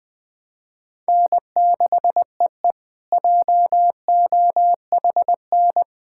Morse code, heard as a keyed tone, about 20 wpm, 700 Hz. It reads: N6EE JOHN
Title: K1USN SST QSO Practice - 20wpm